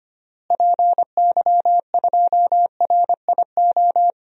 Morse code, heard as a keyed tone, about 25 wpm, 700 Hz. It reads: PY2RIO
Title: DXCC-100 Call Signs - 6 Characters - Repeated 3x - 25wpm